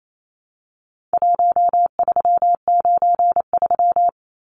Morse code, heard as a keyed tone, about 28 wpm, 700 Hz. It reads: 1393
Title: Numbers - 4-Digit 28wpm